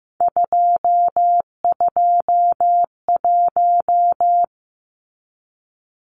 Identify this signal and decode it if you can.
Morse code, heard as a keyed tone, about 15 wpm, 700 Hz. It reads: 221
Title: Numbers - 3-Digit 15wpm